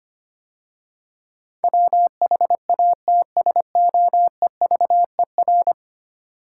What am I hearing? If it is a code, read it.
Morse code, heard as a keyed tone, about 25 wpm, 700 Hz. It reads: WHATSOEVER
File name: 10 Letter Words - First 3 Letters Given - 25wpm